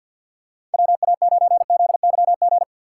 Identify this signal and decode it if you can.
Morse code, heard as a keyed tone, about 52 wpm, 700 Hz. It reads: WA9ZYG
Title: USA Call Signs - 2x Character Spacing - 50wpm